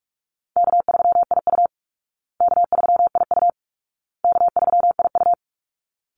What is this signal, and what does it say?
Morse code, heard as a keyed tone, about 44 wpm, 700 Hz. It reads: K3IU K3IU K3IU
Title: Easy Call Signs Repeated 3x - 45wpm